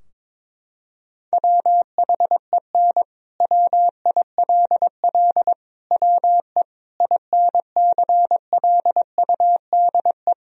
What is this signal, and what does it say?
Morse code, heard as a keyed tone, about 21 wpm, 700 Hz. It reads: WHEN WILL WE INCLUDE
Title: Sentences from Top 1000 Words - Review Part 3 - 22wpm